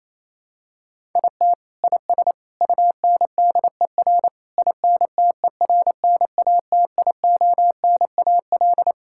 Morse code, heard as a keyed tone, about 28 wpm, 700 Hz. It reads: IT IS UNDER INTERNATIONAL
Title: Sentences from Top 1000 Words - Review Part 6 - 28wpm